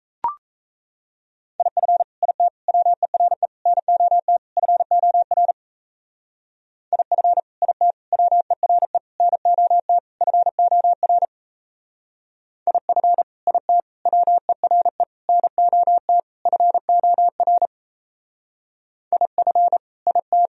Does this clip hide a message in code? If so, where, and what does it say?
Yes: Morse code from 1.6 to 20.5 s, heard as a keyed tone, about 37 wpm, 700 Hz. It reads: IF IT WERE NOT FOR IF IT WERE NOT FOR IF IT WERE NOT FOR IF IT